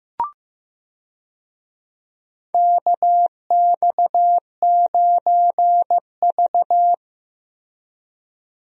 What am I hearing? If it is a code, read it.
Morse code, heard as a keyed tone, about 15 wpm, 700 Hz. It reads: KX9V
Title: USA Call Signs - 4 Characters - Repeated 2x - 15wpm